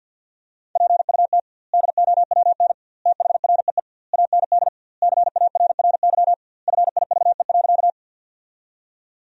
Morse code, heard as a keyed tone, about 50 wpm, 700 Hz. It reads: PUT DOWN THREE AND CARRY FIVE.
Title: 2x Word-Spacing - Sentences from Top 400 Words - 50wpm